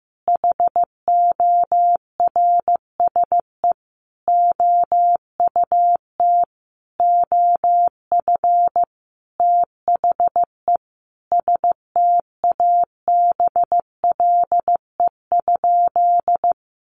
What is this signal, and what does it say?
Morse code, heard as a keyed tone, about 15 wpm, 700 Hz. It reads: HORSE OUT OF THE STABLE?
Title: Sentences from Top 1500 Words 15wpm